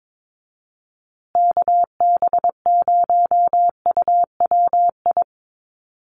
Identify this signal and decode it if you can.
Morse code, heard as a keyed tone, about 22 wpm, 700 Hz. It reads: KB0UWI
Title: USA Call Signs Repeated 2x 22wpm